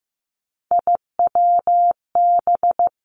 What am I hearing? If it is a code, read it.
Morse code, heard as a keyed tone, about 15 wpm, 700 Hz. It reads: IWB
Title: Three Challenging Letter-Number Combinations 15wpm